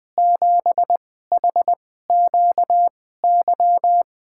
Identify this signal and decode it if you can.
Morse code, heard as a keyed tone, about 20 wpm, spaced as for 14 wpm, 700 Hz. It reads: 7HQY
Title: USA Call Signs - 2x Character Spacing - 20wpm